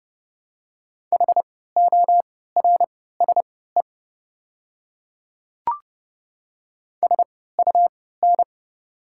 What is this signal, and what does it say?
Morse code, heard as a keyed tone, about 30 wpm, spaced as for 17 wpm, 700 Hz. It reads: HORSE SUN
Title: 3x Character Spacing - Top 300 Words - 30wpm